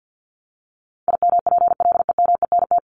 Morse code, heard as a keyed tone, about 50 wpm, 700 Hz. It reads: IMPLEMENT
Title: Top 1501-1750 Words 50wpm